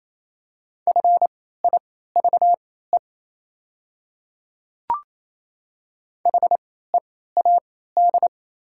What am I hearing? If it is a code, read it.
Morse code, heard as a keyed tone, about 27 wpm, spaced as for 16 wpm, 700 Hz. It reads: FIVE HEAD
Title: 3x Character Spacing - Top 500 Words - 28wpm